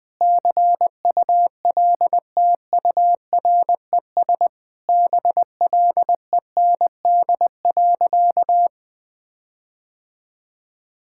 Morse code, heard as a keyed tone, about 21 wpm, 700 Hz. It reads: CULTURES BLEND.
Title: Sentences with 4-Word Phrases - 20wpm